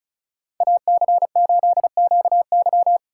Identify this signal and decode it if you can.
Morse code, heard as a keyed tone, about 36 wpm, 700 Hz. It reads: AC8QY
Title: USA Call Signs - 5 Characters - 4x Character Spacing - 35wpm